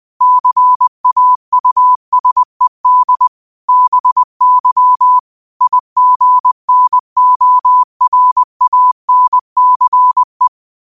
Morse code, heard as a keyed tone, about 20 wpm, 1000 Hz. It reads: CAUSED BY IGNORANCE